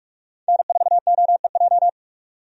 Morse code, heard as a keyed tone, about 41 wpm, 700 Hz. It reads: N4OEJ
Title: USA Call Signs - 5 Characters - 45wpm